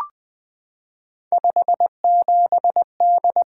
Morse code, heard as a keyed tone, about 20 wpm, 700 Hz. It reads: 57D
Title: Three Letter-Number Combinations - 20wpm